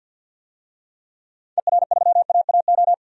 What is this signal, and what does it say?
Morse code, heard as a keyed tone, about 50 wpm, 700 Hz. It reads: ER3AAO
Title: DXCC-100 Call Signs - 6 Characters - 50wpm